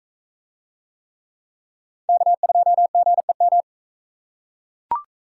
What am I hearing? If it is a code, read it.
Morse code, heard as a keyed tone, about 38 wpm, 700 Hz. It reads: K2GEM